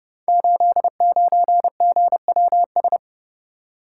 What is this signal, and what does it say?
Morse code, heard as a keyed tone, about 29 wpm, 700 Hz. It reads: 89GWS
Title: Code Groups - 5 Letter:Numbers - 30wpm